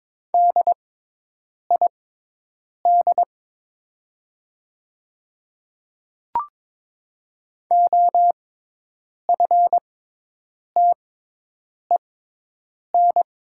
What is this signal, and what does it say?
Morse code, heard as a keyed tone, about 22 wpm, spaced as for 8 wpm, 700 Hz. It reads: DID OFTEN
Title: 6x Character Spacing - Top 300 Words - 22wpm